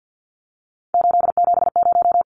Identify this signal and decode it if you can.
Morse code, heard as a keyed tone, about 49 wpm, 700 Hz. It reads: 870
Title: Numbers - 3-Digit 50wpm